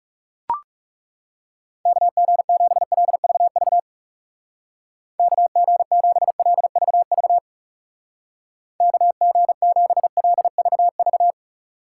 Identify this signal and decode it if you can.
Morse code, heard as a keyed tone, about 38 wpm, 700 Hz. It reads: KG7LVV KG7LVV KG7LVV